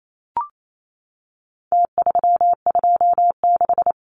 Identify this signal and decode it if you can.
Morse code, heard as a keyed tone, about 28 wpm, 700 Hz. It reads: T326
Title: Numbers - Contest Style - 4-Digit 28wpm